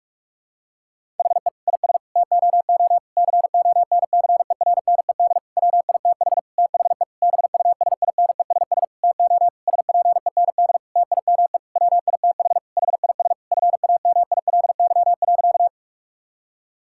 Morse code, heard as a keyed tone, about 42 wpm, 700 Hz. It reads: HE IS TOO CONCERNED WITH THE BUSINESS TO SPEND TIME WITH HIS FAMILY.
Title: Sentences from Top 1000 Words - Review Part 6 - 45wpm